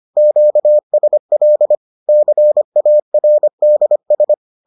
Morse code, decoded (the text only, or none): QSL CARDS